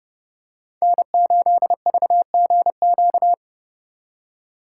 Morse code, heard as a keyed tone, about 30 wpm, 700 Hz. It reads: N8VGQ